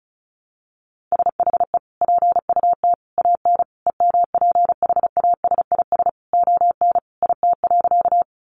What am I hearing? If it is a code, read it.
Morse code, heard as a keyed tone, about 34 wpm, 700 Hz. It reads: SHE PUT AN EMPHASIS ON IT.
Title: Sentences from Top 1400 Words 35wpm